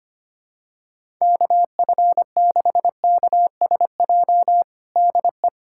Morse code, heard as a keyed tone, about 25 wpm, 700 Hz. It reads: KF6KSJ DE
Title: USA Callsign DE Callsign 25wpm